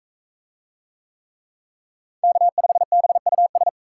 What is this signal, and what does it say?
Morse code, heard as a keyed tone, about 40 wpm, 700 Hz. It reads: K5BUS